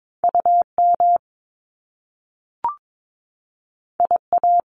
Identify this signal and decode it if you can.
Morse code, heard as a keyed tone, about 22 wpm, 700 Hz. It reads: UM IA
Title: Top Two Letter Combinations in Words 22wpm